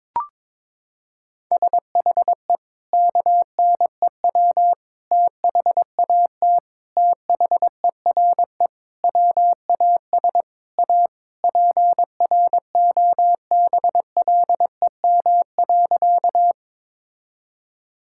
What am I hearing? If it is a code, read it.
Morse code, heard as a keyed tone, about 21 wpm, 700 Hz. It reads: SHE KNEW THAT THERE WAS A PROBLEM.
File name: practice / 22wpm / Sentences with 4-Word Phrases - 22wpm